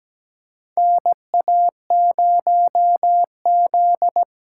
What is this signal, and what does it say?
Morse code, heard as a keyed tone, about 17 wpm, 700 Hz. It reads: NA0Z
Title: USA Call Signs - 4 Characters - 4x Character Spacing - 17wpm